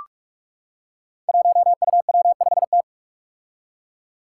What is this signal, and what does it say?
Morse code, heard as a keyed tone, about 45 wpm, 700 Hz. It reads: 1UW5T
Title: Five Challenging Letter-Number Combinations 45wpm